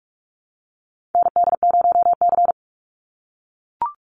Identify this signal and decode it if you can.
Morse code, heard as a keyed tone, about 44 wpm, 700 Hz. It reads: ND0C
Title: NA SOTA Chaser Callsigns 45wpm